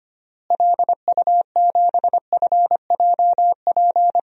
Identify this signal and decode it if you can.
Morse code, heard as a keyed tone, about 25 wpm, 700 Hz. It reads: LU7FJP
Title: DXCC-100 Call Signs - 6 Characters - Repeated 3x - 25wpm